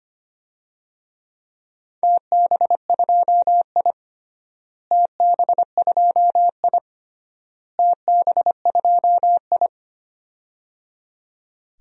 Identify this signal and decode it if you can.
Morse code, heard as a keyed tone, about 25 wpm, 700 Hz. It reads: TB2I TB2I TB2I
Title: DXCC-100 Call Signs - 4 Characters - Repeated 3x - 25wpm